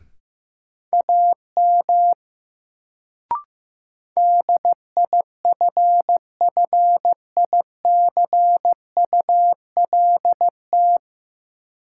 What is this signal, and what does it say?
Morse code, heard as a keyed tone, about 15 wpm, 700 Hz. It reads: AM DIFFICULT
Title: Sentences from Top 600 Words - New Word Drill - 15wpm